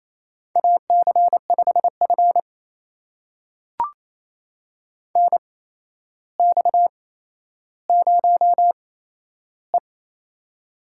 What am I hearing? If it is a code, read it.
Morse code, heard as a keyed tone, about 28 wpm, spaced as for 12 wpm, 700 Hz. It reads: AC5F NX0E
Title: USA Call Signs - 4 Characters - 8x Character Spacing - 28wpm